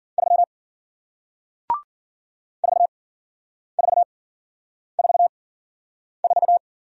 Morse code, heard as a keyed tone, about 54 wpm, spaced as for 9 wpm, 700 Hz. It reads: 3 4444